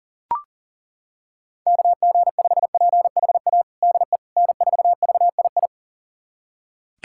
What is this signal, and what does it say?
Morse code, heard as a keyed tone, about 40 wpm, 700 Hz. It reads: KG5PHA DE N4VII
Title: USA Callsign DE Callsign 40wpm